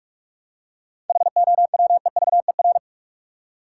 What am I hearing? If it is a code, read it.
Morse code, heard as a keyed tone, about 42 wpm, 700 Hz. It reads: HOWEVER
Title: Seven Letter Words - 45wpm